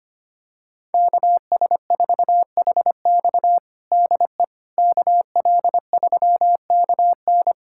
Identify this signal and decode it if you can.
Morse code, heard as a keyed tone, about 25 wpm, 700 Hz. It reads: KS4HX DE KL3KN
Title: USA Callsign DE Callsign - 5 Characters - 25wpm